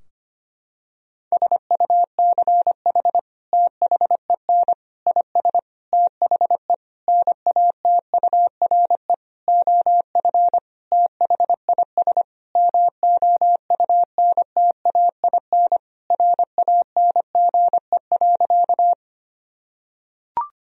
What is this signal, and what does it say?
Morse code, heard as a keyed tone, about 25 wpm, 700 Hz. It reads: SUCH THEN IS THE NATURE OF THIS MOUNTAIN RANGE.
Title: Sentences from Top 1000 Words - Review Part 4 - 25wpm